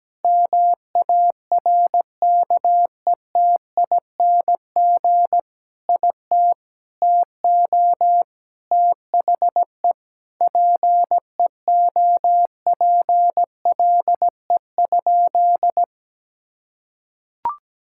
Morse code, heard as a keyed tone, about 17 wpm, 700 Hz. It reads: MARKETING IT TO THE PEOPLE?
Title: Sentences from Top 1000 Words - Review Part 5 - 17wpm